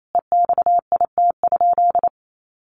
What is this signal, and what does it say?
Morse code, heard as a keyed tone, about 28 wpm, 700 Hz. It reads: EXIT?
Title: Sentences from Top 1100 Words 28wpm